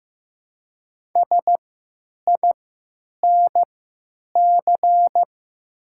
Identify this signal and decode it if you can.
Morse code, heard as a keyed tone, about 15 wpm, spaced as for 8 wpm, 700 Hz. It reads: SINC
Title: 3x Character Spacing - Top 300 Words - 15wpm